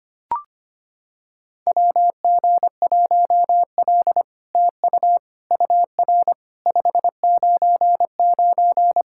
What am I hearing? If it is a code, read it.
Morse code, heard as a keyed tone, about 25 wpm, 700 Hz. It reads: WG1L TU UR 599
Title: SOTA QSO Practice 25wpm